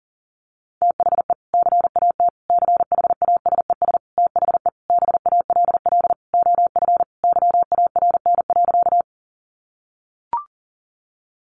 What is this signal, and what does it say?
Morse code, heard as a keyed tone, about 40 wpm, 700 Hz. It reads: THE CAT CHASES THE BALL OF YARN.
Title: Additional Easy Phrases 40wpm